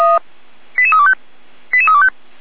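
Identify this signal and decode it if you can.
Five-tone selective call (ZVEI), heard as quick runs of five tones, tones of about 70 ms. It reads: 88325 88325
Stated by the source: Received on SDR, ZVEI five-tone code 88283, 455 channel.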